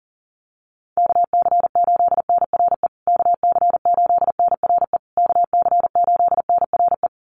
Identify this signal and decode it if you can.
Morse code, heard as a keyed tone, about 40 wpm, 700 Hz. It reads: KC8NRE KC8NRE KC8NRE
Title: USA Call Signs Repeated 3x 40wpm